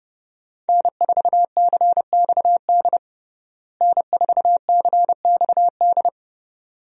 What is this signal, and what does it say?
Morse code, heard as a keyed tone, about 30 wpm, 700 Hz. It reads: N4CXD N4CXD
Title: USA Call Signs - 5 Characters - Repeated 2x - 30wpm